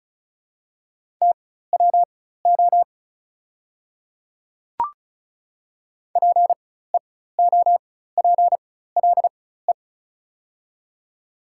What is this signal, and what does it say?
Morse code, heard as a keyed tone, about 36 wpm, spaced as for 16 wpm, 690 Hz. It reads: TWO PEOPLE
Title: 4x Character Spacing - Top 100 Words - 35wpm